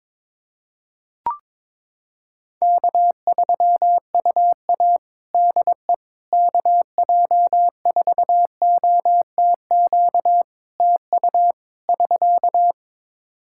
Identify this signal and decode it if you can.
Morse code, heard as a keyed tone, about 21 wpm, 700 Hz. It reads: K3UA DE KJ4OTQ TU <SK>
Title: Practice QSO - Exchange 4 - Ending - 22wpm